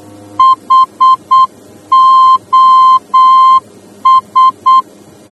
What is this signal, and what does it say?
Morse code, heard as a keyed tone, about 8 wpm, 1040 Hz. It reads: HOS
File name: germany / mx / HOS-D